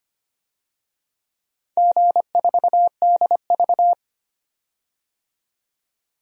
Morse code, heard as a keyed tone, about 25 wpm, 700 Hz. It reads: G4DV
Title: DXCC-100 Call Signs - 4 Characters - 25wpm